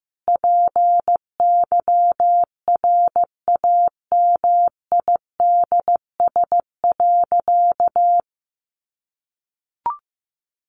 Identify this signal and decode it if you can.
Morse code, heard as a keyed tone, about 15 wpm, 700 Hz. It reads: PYRAMIDS.
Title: A Sound Of Thunder 15wpm